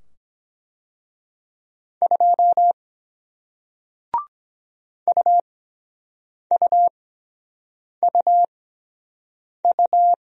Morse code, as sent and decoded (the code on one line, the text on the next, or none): ..--- / ..- ..- ..- ..-
2 UUUU